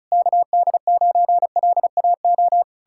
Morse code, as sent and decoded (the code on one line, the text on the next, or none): -.- -.. ----. .-.. .- ---
KD9LAO